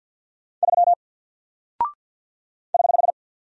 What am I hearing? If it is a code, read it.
Morse code, heard as a keyed tone, about 48 wpm, 700 Hz. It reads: 3 <HH>
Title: Single Character (Letter:Number:Punctuation:Prosign) Alt 50wpm